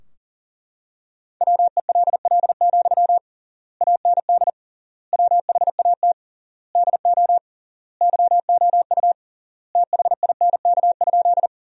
Morse code, heard as a keyed tone, about 40 wpm, 700 Hz. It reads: WELL, AND WHAT DO YOU THINK?
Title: 3x Word-Spacing - Sentences from Top 200 Words - 40wpm